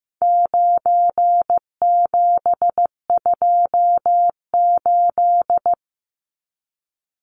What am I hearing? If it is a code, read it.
Morse code, heard as a keyed tone, about 15 wpm, 700 Hz. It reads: 9728
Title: Numbers - 4-Digit 15wpm